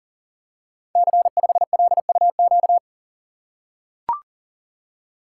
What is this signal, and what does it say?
Morse code, heard as a keyed tone, about 38 wpm, 700 Hz. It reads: C5LUQ
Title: Code Groups - 5 Letter:Numbers - 40wpm